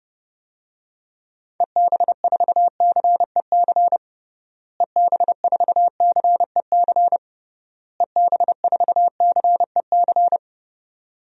Morse code, heard as a keyed tone, about 30 wpm, 700 Hz. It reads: EB4CEC EB4CEC EB4CEC
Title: DXCC-100 Call Signs - 6 Characters - Repeated 3x - 30wpm